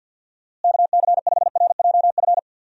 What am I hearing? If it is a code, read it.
Morse code, heard as a keyed tone, about 50 wpm, 700 Hz. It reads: KC5RJF
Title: USA Call Signs - 6 Characters - Repeated 2x - 50wpm